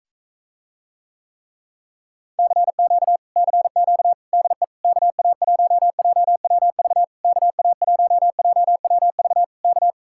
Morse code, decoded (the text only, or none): CQ CQ DE KA1JWV KA1JWV K